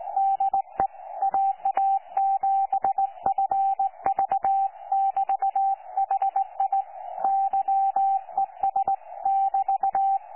Morse code, heard as a keyed tone, about 17 wpm, 780 Hz. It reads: DE AA7FV<BT>HI YES<BT>
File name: 12 39Z 18088.2kHz utah aa7fv k9tjl